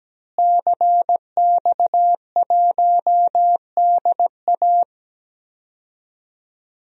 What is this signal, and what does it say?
Morse code, heard as a keyed tone, about 17 wpm, 700 Hz. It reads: CX1DA
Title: DXCC-100 Call Signs - 5 Characters - Repeated 3x - 17wpm